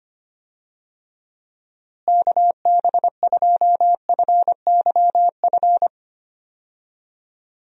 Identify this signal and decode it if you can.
Morse code, heard as a keyed tone, about 25 wpm, 700 Hz. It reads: KB2FYF
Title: USA Call Signs - 6 Characters - Repeated 3x - 25wpm